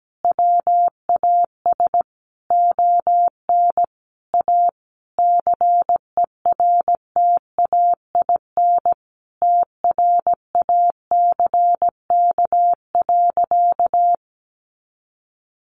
Morse code, heard as a keyed tone, about 17 wpm, 700 Hz. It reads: WAS ON A CERTAIN TRACK.
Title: Sentences from Top 1100 Words 17wpm